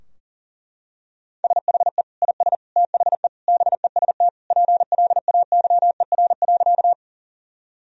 Morse code, heard as a keyed tone, about 40 wpm, 700 Hz. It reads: SHE IS THE BEST PLAYER.
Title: Sentences from Top 1000 Words - Review Part 4 - 40wpm